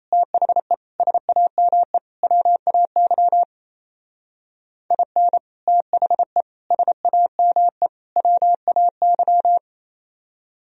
Morse code, heard as a keyed tone, about 36 wpm, 700 Hz. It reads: THE SAME WAY IN THE SAME WAY